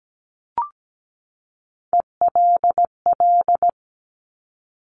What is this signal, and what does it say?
Morse code, heard as a keyed tone, about 17 wpm, 700 Hz. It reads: ELL
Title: Top Three Letter Combinations in Words 17wpm